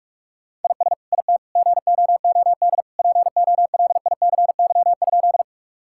Morse code, heard as a keyed tone, about 45 wpm, 700 Hz. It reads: IS IT GOOD POLICY?
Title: Sentences from Top 1000 Words - Review Part 5 - 45wpm